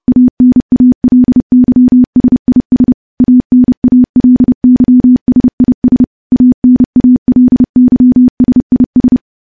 Morse code, heard as a keyed tone, about 30 wpm, 250 Hz. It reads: ANALYSIS ANALYSIS ANALYSIS
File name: morse / morse-training-30WPM